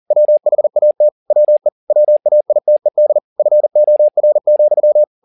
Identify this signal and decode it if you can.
Morse code, heard as a keyed tone, about 40 wpm, 600 Hz. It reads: WHAT WE WAITED FOR,
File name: the old man and the sea 2 40wpm